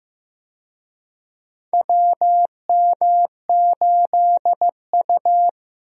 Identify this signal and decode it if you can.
Morse code, heard as a keyed tone, about 16 wpm, 700 Hz. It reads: WM8U
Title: USA Call Signs - 4 Characters - Repeated 3x - 15wpm